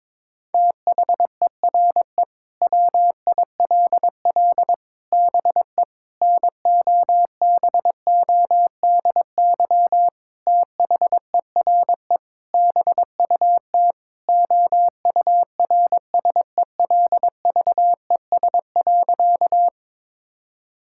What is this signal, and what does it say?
Morse code, heard as a keyed tone, about 22 wpm, 700 Hz. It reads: THERE WILL BE NOBODY THERE BUT OURSELVES.
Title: Sentences from Top 1600 Words 22wpm